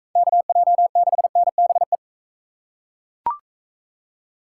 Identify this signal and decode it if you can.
Morse code, heard as a keyed tone, about 40 wpm, 700 Hz. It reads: KJ6NBE